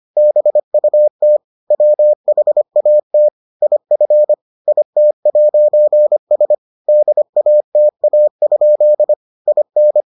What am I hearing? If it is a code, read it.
Morse code, heard as a keyed tone, about 25 wpm, 600 Hz. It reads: BUT WHAT IF IT'S DATA? IN